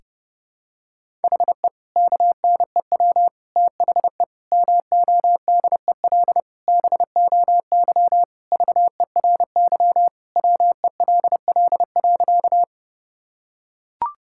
Morse code, heard as a keyed tone, about 30 wpm, 700 Hz. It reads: HE KNEW THE MODEL BOY VERY WELL.
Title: Sentences from Top 1000 Words - Review Part 5 - 30wpm